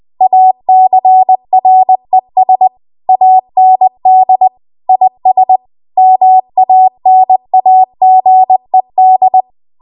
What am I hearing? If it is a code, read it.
Morse code, heard as a keyed tone, about 20 wpm, 750 Hz. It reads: ACRES AND IS MANAGED
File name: Archive / 20 WPM / 20WPM